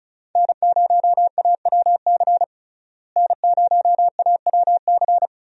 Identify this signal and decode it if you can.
Morse code, heard as a keyed tone, about 34 wpm, 700 Hz. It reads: N0AWC N0AWC
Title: USA Call Signs - 5 Characters - Repeated 2x - 35wpm